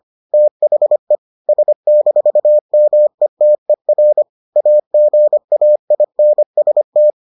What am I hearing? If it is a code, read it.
Morse code, heard as a keyed tone, about 25 wpm, 600 Hz. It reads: THE S-METER AGAINST